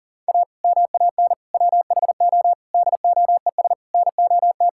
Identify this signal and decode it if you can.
Morse code, heard as a keyed tone, about 40 wpm, 700 Hz. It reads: A MAN WHO DOES NOT
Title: Sentences from Top 1600 Words 40wpm